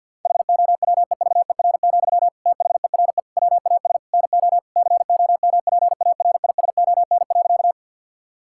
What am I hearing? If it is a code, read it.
Morse code, heard as a keyed tone, about 50 wpm, 700 Hz. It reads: HOWEVER, THERE WAS NO COMPARISON.
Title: Sentences from Top 1600 Words 50wpm